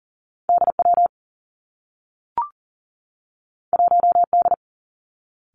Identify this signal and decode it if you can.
Morse code, heard as a keyed tone, about 40 wpm, 700 Hz. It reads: DW 1D
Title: Two Challenging Letter-Number Combinations 40wpm